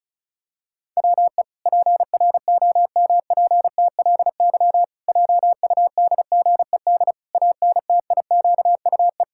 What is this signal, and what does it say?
Morse code, heard as a keyed tone, about 34 wpm, 700 Hz. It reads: WE PROMPTLY JUDGED ANTIQUE
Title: Pangram Challenge v2 - 35wpm